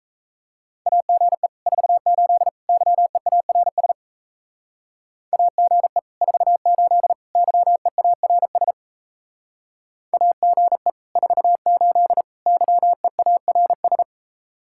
Morse code, heard as a keyed tone, about 34 wpm, 700 Hz. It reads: AGE 48 YEARS AGE 48 YEARS AGE 48 YEARS
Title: Speed-Racing Common QSO Phrases Encore - 28wpm